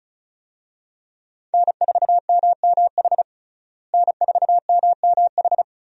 Morse code, heard as a keyed tone, about 36 wpm, 700 Hz. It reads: N4MMH N4MMH
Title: USA Call Signs - 5 Characters - Repeated 3x - 35wpm